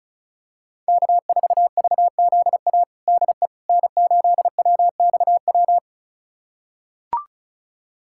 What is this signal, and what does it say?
Morse code, heard as a keyed tone, about 36 wpm, 700 Hz. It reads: K4VZA DE N8WXW
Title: USA Callsign DE Callsign - 5 Characters - 35wpm